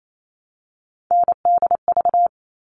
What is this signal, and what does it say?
Morse code, heard as a keyed tone, about 28 wpm, 700 Hz. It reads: NDV
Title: All Cut Numbers - 3-Digit 28wpm